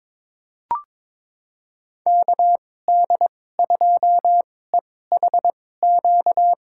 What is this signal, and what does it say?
Morse code, heard as a keyed tone, about 22 wpm, spaced as for 16 wpm, 700 Hz. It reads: KD2EHQ
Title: USA Call Signs - 6 Characters - 2x Character Spacing - 22wpm